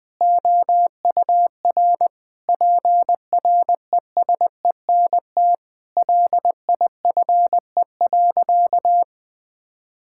Morse code, heard as a keyed tone, about 20 wpm, 700 Hz. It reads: OUR PRESENT LIFE.
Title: Sentences from Top 1000 Words - Review Part 6 - 20wpm